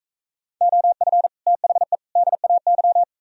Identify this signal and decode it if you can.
Morse code, heard as a keyed tone, about 43 wpm, 700 Hz. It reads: OF THE DAY